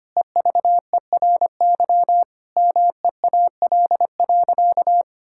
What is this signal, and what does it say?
Morse code, heard as a keyed tone, about 25 wpm, 700 Hz. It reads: EVERY MEAL.
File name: practice / 25wpm / Difficult Sentences with Hints - 25wpm